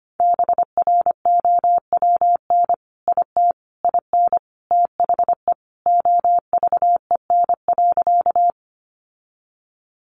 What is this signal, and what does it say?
Morse code, heard as a keyed tone, about 25 wpm, 700 Hz. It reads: BROWN IT IN THE OVEN.
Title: Sentences from Top 1200 Words 25wpm